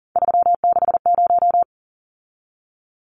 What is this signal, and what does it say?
Morse code, heard as a keyed tone, about 40 wpm, 700 Hz. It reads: 360
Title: Numbers - 3-Digit 40wpm